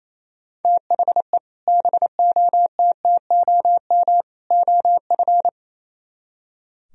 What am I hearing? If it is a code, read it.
Morse code, heard as a keyed tone, about 37 wpm, 700 Hz. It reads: THE BOTTOM OF